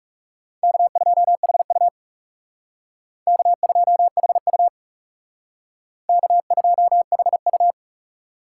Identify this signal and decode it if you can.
Morse code, heard as a keyed tone, about 39 wpm, 700 Hz. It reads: K2HU K2HU K2HU